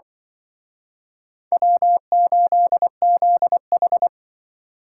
Morse code, heard as a keyed tone, about 23 wpm, 700 Hz. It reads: W8ZH
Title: Speed Racing - USA 1x2 Call Signs 20wpm